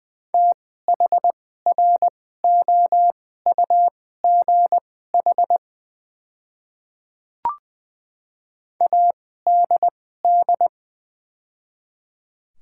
Morse code, heard as a keyed tone, about 20 wpm, spaced as for 14 wpm, 690 Hz. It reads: THROUGH ADD
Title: 2x Character Spacing - Top 500 Words - 20wpm